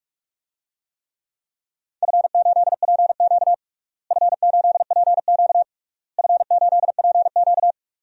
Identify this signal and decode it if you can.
Morse code, heard as a keyed tone, about 48 wpm, 700 Hz. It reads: F8PQ F8PQ F8PQ
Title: DXCC-100 Call Signs - 4 Characters - Repeated 3x - 45wpm